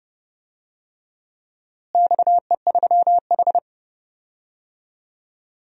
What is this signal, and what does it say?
Morse code, heard as a keyed tone, about 32 wpm, 700 Hz. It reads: XE3H
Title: DXCC-100 Call Signs - 4 Characters - Repeated 2x - 30wpm